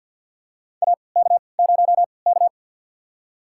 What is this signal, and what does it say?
Morse code, heard as a keyed tone, about 50 wpm, spaced as for 28 wpm, 700 Hz. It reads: AK0K
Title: USA Call Signs - 4 Characters - 3x Character Spacing - 50wpm